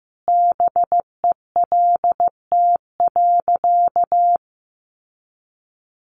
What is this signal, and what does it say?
Morse code, heard as a keyed tone, about 14 wpm, 700 Hz. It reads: BELT.
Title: Sentences from Top 1400 Words 15wpm